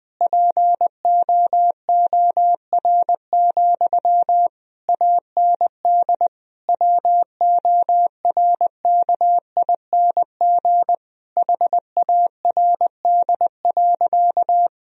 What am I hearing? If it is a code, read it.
Morse code, heard as a keyed tone, about 20 wpm, 700 Hz. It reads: POOR, AND WORKING HARD.
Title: Sentences from Top 1000 Words - Review Part 4 - 20wpm